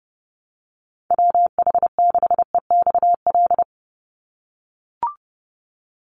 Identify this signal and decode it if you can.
Morse code, heard as a keyed tone, about 30 wpm, 700 Hz. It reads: WH6EXL
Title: USA Call Signs 30wpm